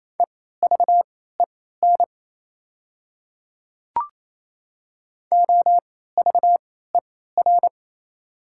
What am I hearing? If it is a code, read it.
Morse code, heard as a keyed tone, about 27 wpm, spaced as for 16 wpm, 700 Hz. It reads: EVEN OVER
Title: 3x Character Spacing - Top 500 Words - 28wpm